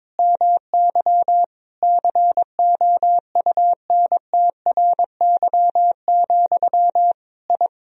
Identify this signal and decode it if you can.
Morse code, heard as a keyed tone, about 22 wpm, 700 Hz. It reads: MY COUNTRY, I
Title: Sentences from Top 1000 Words - Review Part 2 - 22wpm